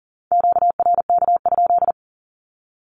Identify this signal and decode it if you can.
Morse code, heard as a keyed tone, about 40 wpm, 700 Hz. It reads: QRK?
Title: Q-Codes 40wpm